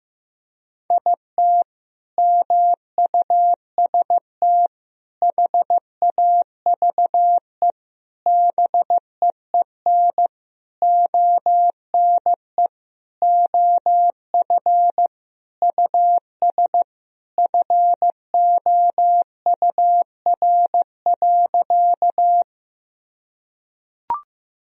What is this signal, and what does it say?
Morse code, heard as a keyed tone, about 15 wpm, 700 Hz. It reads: IT MUST HAVE BEEN ONE OF US FOUR.
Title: Sentences from Top 1000 Words - Review Part 1 - 15wpm